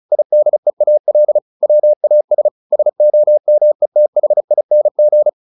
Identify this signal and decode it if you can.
Morse code, heard as a keyed tone, about 34 wpm, 600 Hz. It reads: IDEAL WAS SOMETHING